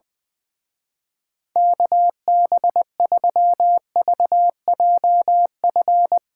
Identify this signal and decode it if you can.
Morse code, heard as a keyed tone, about 23 wpm, 700 Hz. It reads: KB3VJF
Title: Speed Racing - USA 2x3 Call Signs 20wpm